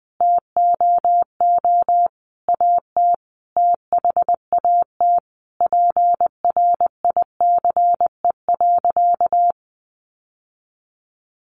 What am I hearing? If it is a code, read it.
Morse code, heard as a keyed tone, about 20 wpm, 700 Hz. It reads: TOO AT THAT PRICE.
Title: Sentences from Top 1200 Words 20wpm